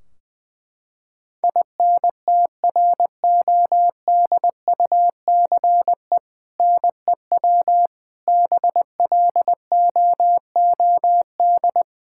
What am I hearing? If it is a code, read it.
Morse code, heard as a keyed tone, about 21 wpm, 700 Hz. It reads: INTRODUCE NEW BLOOD
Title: Sentences from Top 1600 Words 20wpm